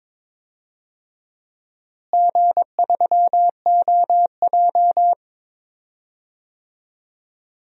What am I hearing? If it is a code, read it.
Morse code, heard as a keyed tone, about 22 wpm, 700 Hz. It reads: G3OJ
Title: DXCC-100 Call Signs - 4 Characters - Repeated 2x - 22wpm